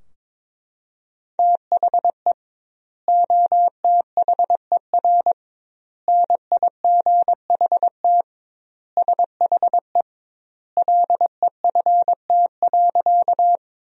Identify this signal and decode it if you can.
Morse code, heard as a keyed tone, about 22 wpm, 700 Hz. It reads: THE OTHER NIGHT SHE LEFT.
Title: 2x Word-Spacing - Sentences from Top 300 Words - 22wpm